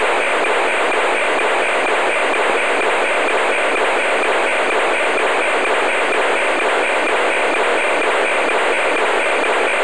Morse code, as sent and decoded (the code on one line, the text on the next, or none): none